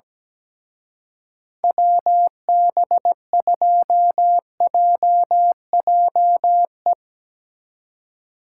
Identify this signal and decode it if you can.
Morse code, heard as a keyed tone, about 17 wpm, 700 Hz. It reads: WB2JJE
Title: Speed Racing - USA 2x3 Call Signs 17wpm